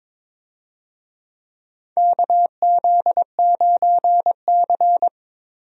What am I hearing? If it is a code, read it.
Morse code, heard as a keyed tone, about 22 wpm, 700 Hz. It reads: KZ9C
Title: USA Call Signs - 4 Characters - 22wpm